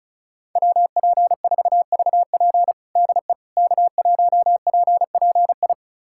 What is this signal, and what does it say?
Morse code, heard as a keyed tone, about 34 wpm, 700 Hz. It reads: WP4VP DE K1PPI
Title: USA Callsign DE Callsign - 5 Characters - 35wpm